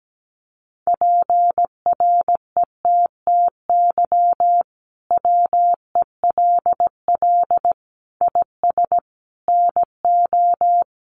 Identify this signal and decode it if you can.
Morse code, heard as a keyed tone, about 17 wpm, 700 Hz. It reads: PRETTY WELL IS NO
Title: Sentences from Top 1000 Words 17wpm